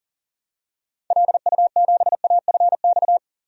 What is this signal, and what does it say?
Morse code, heard as a keyed tone, about 40 wpm, 700 Hz. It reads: LU7AFX
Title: DXCC-100 Call Signs - 6 Characters - 40wpm